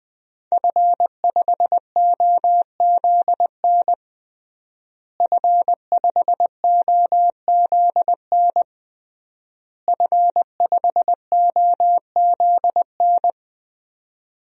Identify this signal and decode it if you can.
Morse code, heard as a keyed tone, about 20 wpm, 700 Hz. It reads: F5OZN F5OZN F5OZN
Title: DXCC-100 Call Signs - 5 Characters - Repeated 3x - 20wpm